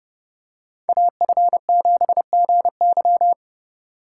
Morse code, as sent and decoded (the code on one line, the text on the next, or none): .- ..-. --... --. -.--
AF7GY